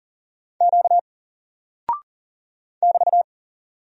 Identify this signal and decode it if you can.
Morse code, heard as a keyed tone, about 39 wpm, 700 Hz. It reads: Q <BT>
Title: Single Character (Letter:Number:Punctuation:Prosign) 40wpm